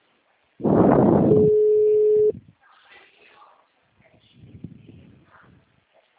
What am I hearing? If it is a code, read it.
Ringback tone (Europe, 425 Hz), with speech.